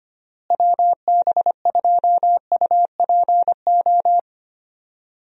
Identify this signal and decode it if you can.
Morse code, heard as a keyed tone, about 25 wpm, 700 Hz. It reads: WB2UPO
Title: USA Call Signs - 6 Characters - Repeated 3x - 25wpm